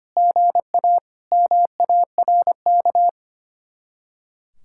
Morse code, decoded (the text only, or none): GA MARK